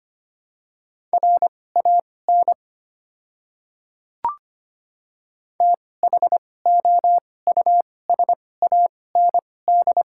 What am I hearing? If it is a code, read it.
Morse code, heard as a keyed tone, about 25 wpm, spaced as for 18 wpm, 690 Hz. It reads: RAN THOUSAND
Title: 2x Character Spacing - Top 500 Words - 25wpm